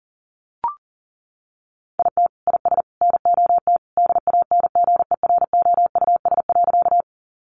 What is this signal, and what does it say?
Morse code, heard as a keyed tone, about 40 wpm, 700 Hz. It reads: IT IS NOT DANGEROUS.
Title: Sentences from Top 1200 Words 40wpm